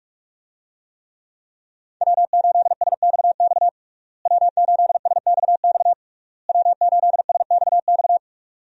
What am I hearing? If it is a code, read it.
Morse code, heard as a keyed tone, about 41 wpm, 700 Hz. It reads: W8SXX W8SXX W8SXX
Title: USA Call Signs - 5 Characters - Repeated 3x - 45wpm